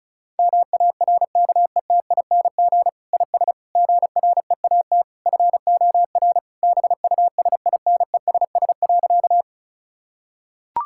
Morse code, heard as a keyed tone, about 36 wpm, 700 Hz. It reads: MARKETING IS GREAT FOR BUSINESS.
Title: Sentences from Top 1000 Words - Review Part 5 - 35wpm